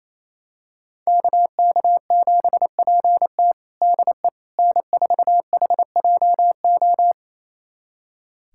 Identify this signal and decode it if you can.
Morse code, heard as a keyed tone, about 28 wpm, 700 Hz. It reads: KK7PT DE N4HJO
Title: USA Callsign DE Callsign - 5 Characters - 28wpm